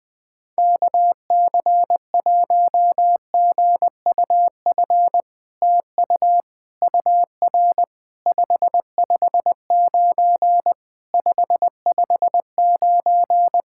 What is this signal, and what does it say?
Morse code, heard as a keyed tone, about 21 wpm, 700 Hz. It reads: KC1GUF TU UR 559 559
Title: SOTA QSO Practice 20wpm